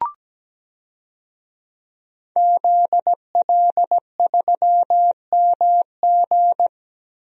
Morse code, heard as a keyed tone, about 17 wpm, 700 Hz. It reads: ZL3MG
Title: DXCC-100 Call Signs - 5 Characters - Repeated 2x - 17wpm